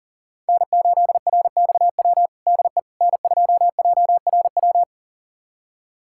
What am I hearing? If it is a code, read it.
Morse code, heard as a keyed tone, about 37 wpm, 700 Hz. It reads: N8RXW DE N2JRW
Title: USA Callsign DE Callsign - 5 Characters - 40wpm